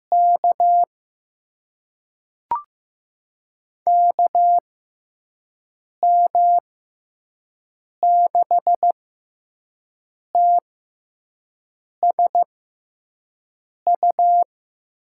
Morse code, heard as a keyed tone, about 15 wpm, spaced as for 5 wpm, 700 Hz. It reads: K KM6TSU
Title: USA Call Signs - 6 Characters - 6x Character Spacing - 15wpm